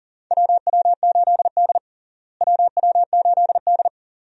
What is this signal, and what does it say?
Morse code, heard as a keyed tone, about 40 wpm, 700 Hz. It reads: WW8D WW8D
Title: USA Call Signs - 4 Characters - Repeated 2x - 40wpm